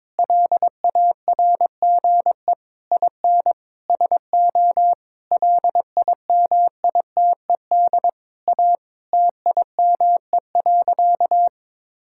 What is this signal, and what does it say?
Morse code, heard as a keyed tone, about 22 wpm, 700 Hz. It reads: LARGE IN SO LIMITED A TIME.
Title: Sentences from Top 1000 Words - Review Part 6 - 22wpm